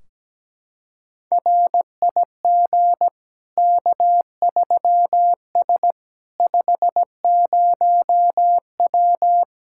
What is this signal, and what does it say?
Morse code, heard as a keyed tone, about 17 wpm, 700 Hz. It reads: RIG K3S 50W
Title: Practice QSO - Exchange 2 - Info - 17wpm